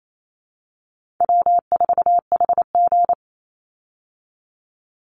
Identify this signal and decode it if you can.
Morse code, heard as a keyed tone, about 28 wpm, 700 Hz. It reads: W4HG
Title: Easy Call Signs Repeated 2x - 28wpm